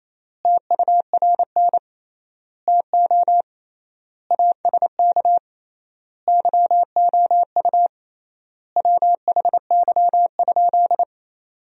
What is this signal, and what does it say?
Morse code, heard as a keyed tone, about 27 wpm, 700 Hz. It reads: TURN TO ASK YOU WHY?
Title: 3x Word-Spacing - Sentences from Top 200 Words - 28wpm